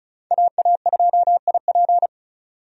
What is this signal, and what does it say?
Morse code, heard as a keyed tone, about 36 wpm, 700 Hz. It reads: AA2IP
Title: USA Call Signs - 5 Characters - 35wpm